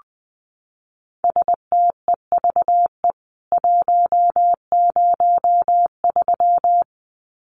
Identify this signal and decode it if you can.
Morse code, heard as a keyed tone, about 20 wpm, 700 Hz. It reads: STEVE 103
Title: CWT Exchange 20wpm